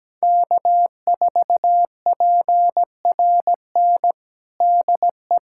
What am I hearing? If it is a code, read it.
Morse code, heard as a keyed tone, about 17 wpm, 700 Hz. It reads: K4PRN DE
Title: USA Callsign DE Callsign - 5 Characters - 17wpm